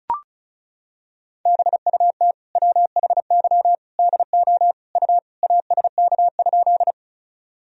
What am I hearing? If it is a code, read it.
Morse code, heard as a keyed tone, about 36 wpm, 700 Hz. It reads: BUT WHY DO U ASK?
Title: Sentences from Top 200 Words and Abbreviations 35wpm